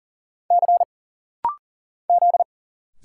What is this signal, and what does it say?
Morse code, heard as a keyed tone, about 40 wpm, 700 Hz. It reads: C Z
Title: Single Letter-Number - Rapid-Fire 40wpm